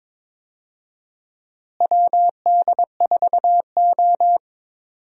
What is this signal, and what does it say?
Morse code, heard as a keyed tone, about 22 wpm, 700 Hz. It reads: WD4O
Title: USA Call Signs - 4 Characters - Repeated 3x - 22wpm